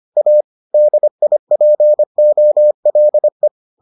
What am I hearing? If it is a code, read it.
Morse code, heard as a keyed tone, about 25 wpm, 600 Hz. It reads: A DIPOLE